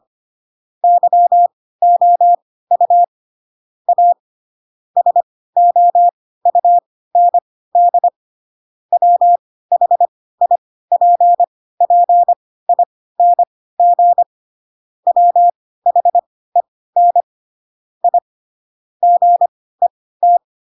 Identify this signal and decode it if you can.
Morse code, heard as a keyed tone, about 25 wpm, spaced as for 16 wpm, 700 Hz. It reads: YOU A SOUND WHIPPING WHEN I GET
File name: Pinocchio / Pinocchio ch7 10 25 ik1hge